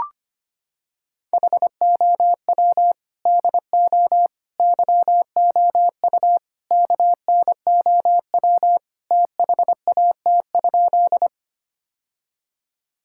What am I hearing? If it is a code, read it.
Morse code, heard as a keyed tone, about 25 wpm, 700 Hz. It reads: HOW DO YOU KNOW THAT?
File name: practice / 25wpm / Sentences with 4-Word Phrases - 25wpm